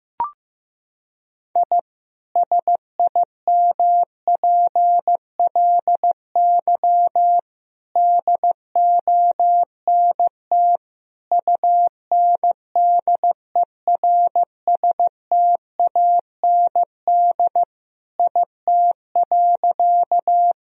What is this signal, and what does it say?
Morse code, heard as a keyed tone, about 15 wpm, 700 Hz. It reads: I SIMPLY DONT UNDERSTAND IT.
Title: Sentences from Top 1000 Words - Review Part 4 - 15wpm